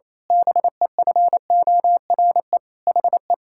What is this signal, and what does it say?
Morse code, heard as a keyed tone, about 28 wpm, 700 Hz. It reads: BEFORE HE
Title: Pangram Challenge v2 - 28wpm